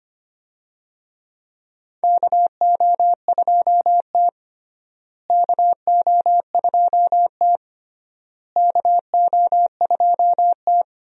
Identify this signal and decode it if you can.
Morse code, heard as a keyed tone, about 25 wpm, 700 Hz. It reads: KO2T KO2T KO2T
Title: USA Call Signs - 4 Characters - Repeated 3x - 25wpm